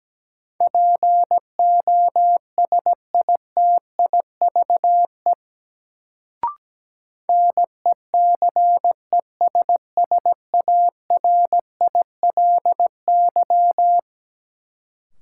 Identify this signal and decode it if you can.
Morse code, heard as a keyed tone, about 17 wpm, 700 Hz. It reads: POSITIVE NECESSARILY
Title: Sentences from Top 1400 Words - New Word Drill Alt - 17wpm